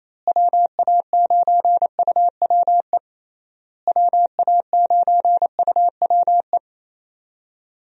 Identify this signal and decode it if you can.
Morse code, heard as a keyed tone, about 28 wpm, 700 Hz. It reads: WA9UWE WA9UWE
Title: USA Call Signs - 6 Characters - Repeated 3x - 28wpm